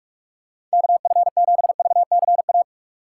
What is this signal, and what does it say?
Morse code, heard as a keyed tone, about 43 wpm, 700 Hz. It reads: KF7VCA
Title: USA Call Signs - 6 Characters - 2x Character Spacing - 45wpm